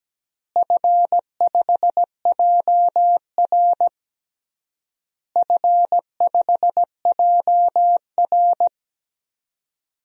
Morse code, heard as a keyed tone, about 17 wpm, 700 Hz. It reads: F5JR F5JR
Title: DXCC-100 Call Signs - 4 Characters - Repeated 3x - 17wpm